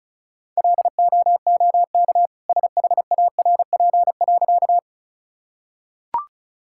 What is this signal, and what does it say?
Morse code, heard as a keyed tone, about 34 wpm, 700 Hz. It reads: LOOK SHARP.
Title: Sentences from Top 1600 Words 35wpm